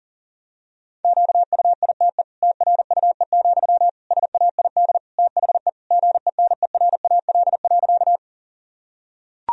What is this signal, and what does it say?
Morse code, heard as a keyed tone, about 40 wpm, 700 Hz. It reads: QUITE TRUE, SAID THE GENERAL.
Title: Sentences from Top 1000 Words - Review Part 3 - 40wpm